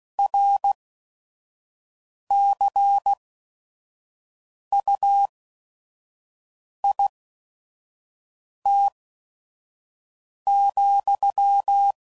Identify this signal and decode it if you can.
Morse code, heard as a keyed tone, about 16 wpm, 800 Hz. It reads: R C U I T ,